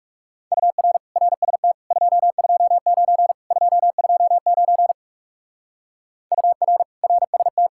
Morse code, heard as a keyed tone, about 37 wpm, 700 Hz. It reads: UR RST 229 229 UR RST
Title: Speed-Racing Common QSO Phrases - 30wpm